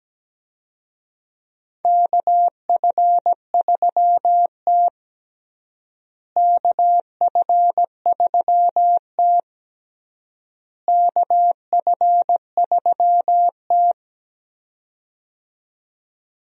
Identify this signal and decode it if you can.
Morse code, heard as a keyed tone, about 17 wpm, 700 Hz. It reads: KF3T KF3T KF3T
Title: USA Call Signs - 4 Characters - Repeated 3x - 17wpm